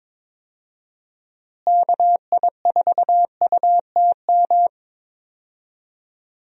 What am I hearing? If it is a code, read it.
Morse code, heard as a keyed tone, about 21 wpm, 700 Hz. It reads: KI4UTM
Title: USA Call Signs - 6 Characters - Repeated 3x - 22wpm